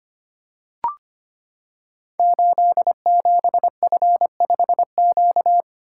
Morse code, heard as a keyed tone, about 25 wpm, 700 Hz. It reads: 87F5Q
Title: Code Groups - 5 Letter:Numbers - 25wpm